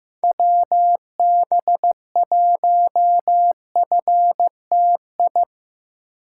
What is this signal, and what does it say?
Morse code, heard as a keyed tone, about 15 wpm, 700 Hz. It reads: WB1FTI
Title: USA Call Signs - 6 Characters - 2x Character Spacing - 15wpm